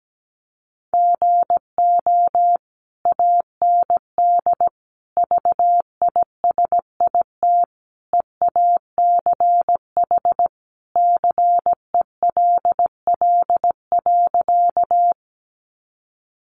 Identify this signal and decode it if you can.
Morse code, heard as a keyed tone, about 17 wpm, 700 Hz. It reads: GO AND VISIT EACH CELL.
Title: Sentences from Top 1100 Words 17wpm